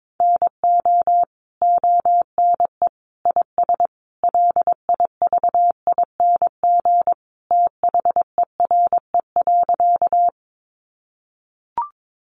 Morse code, heard as a keyed tone, about 22 wpm, 700 Hz. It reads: NO ONE IS LIVING THERE.
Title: Sentences from Top 1100 Words 22wpm